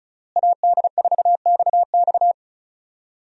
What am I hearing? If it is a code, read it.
Morse code, heard as a keyed tone, about 35 wpm, 700 Hz. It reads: AD4XX
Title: USA Call Signs - 5 Characters - Repeated 3x - 35wpm